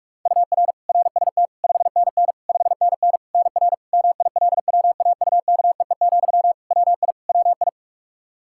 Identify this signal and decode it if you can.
Morse code, heard as a keyed tone, about 46 wpm, 700 Hz. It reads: UR RST 5NN 5NN NR MILWAUKEE, WI WI
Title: Practice QSO - Exchange 1 - Beginning - 45wpm